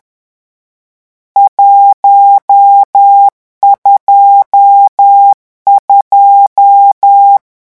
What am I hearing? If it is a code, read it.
Morse code, heard as a keyed tone, about 10 wpm, 800 Hz. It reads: 122
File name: morse mp3 / lecon10-vitesse 10